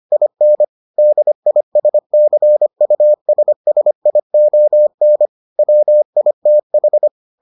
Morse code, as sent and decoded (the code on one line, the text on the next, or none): .. -. / -.. .. ... -.-. ..- ... ... .. --- -. / .-- .. - ....
IN DISCUSSION WITH